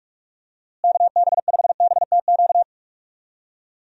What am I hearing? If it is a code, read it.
Morse code, heard as a keyed tone, about 46 wpm, 700 Hz. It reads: KB5BTQ
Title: USA Call Signs - 6 Characters - Repeated 3x - 45wpm